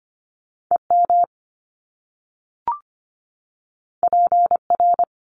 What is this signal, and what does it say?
Morse code, heard as a keyed tone, about 25 wpm, 700 Hz. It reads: EM PR
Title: Top Two Letter Combinations in Words 25wpm